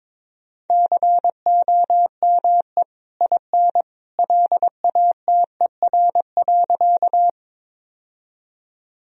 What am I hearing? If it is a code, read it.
Morse code, heard as a keyed tone, about 22 wpm, 700 Hz. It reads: COME IN LATER.
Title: Sentences from Top 1000 Words - Review Part 4 - 22wpm